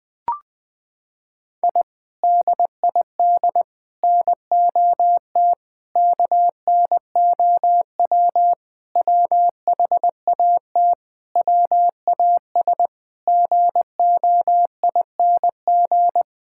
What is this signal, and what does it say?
Morse code, heard as a keyed tone, about 20 wpm, 700 Hz. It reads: I DID NOT KNOW WHAT WAS GOING
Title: Sentences with 4-Word Phrases - 20wpm